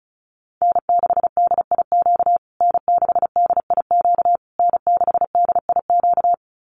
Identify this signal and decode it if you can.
Morse code, heard as a keyed tone, about 34 wpm, 700 Hz. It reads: N6DIQ N6DIQ N6DIQ
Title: USA Call Signs Repeated 3x 35wpm